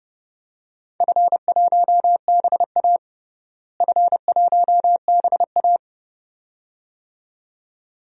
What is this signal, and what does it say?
Morse code, heard as a keyed tone, about 32 wpm, 700 Hz. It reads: F1BA F1BA
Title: DXCC-100 Call Signs - 4 Characters - Repeated 2x - 30wpm